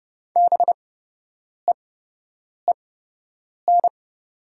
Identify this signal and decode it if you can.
Morse code, heard as a keyed tone, about 30 wpm, spaced as for 8 wpm, 700 Hz. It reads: BEEN
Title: 8x Character Spacing - Top 300 Words - 30wpm